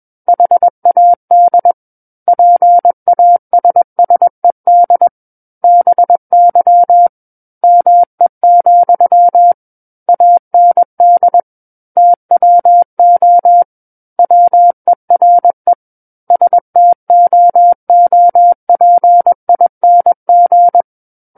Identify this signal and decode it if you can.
Morse code, heard as a keyed tone, about 21 wpm, 700 Hz. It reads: HAD PASSED BY ME, AND TWO WERE STOOPING
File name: Book One / 12-Chapter 12 - 21 WPM